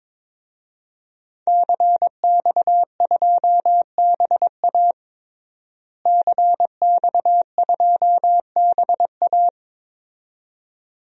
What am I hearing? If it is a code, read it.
Morse code, heard as a keyed tone, about 21 wpm, 700 Hz. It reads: CX2BA CX2BA
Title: DXCC-100 Call Signs - 5 Characters - Repeated 2x - 22wpm